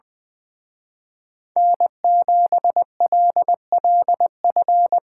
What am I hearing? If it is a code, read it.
Morse code, heard as a keyed tone, about 21 wpm, 700 Hz. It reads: N7LLF
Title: Speed Racing - USA 1x3 Call Signs 20wpm